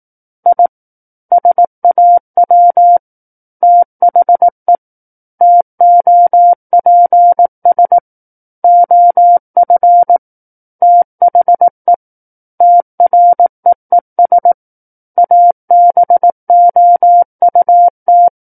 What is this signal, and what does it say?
Morse code, heard as a keyed tone, about 18 wpm, 700 Hz. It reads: I SAW THE TOPS OF THE TREES ABOUT
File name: 18 WPM